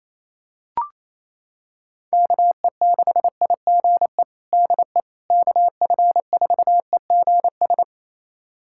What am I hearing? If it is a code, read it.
Morse code, heard as a keyed tone, about 28 wpm, 700 Hz. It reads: KE6IGE DE KF4EGS
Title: USA Callsign DE Callsign - 6 Characters - 28wpm